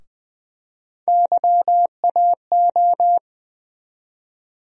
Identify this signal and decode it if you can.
Morse code, heard as a keyed tone, about 20 wpm, 700 Hz. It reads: YAO
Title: USA Call Sign Postfixes - 3 Letters 20wpm